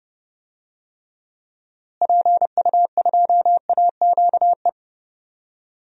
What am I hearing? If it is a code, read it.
Morse code, heard as a keyed tone, about 30 wpm, 700 Hz. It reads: PU2AQE
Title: DXCC-100 Call Signs - 6 Characters - Repeated 2x - 30wpm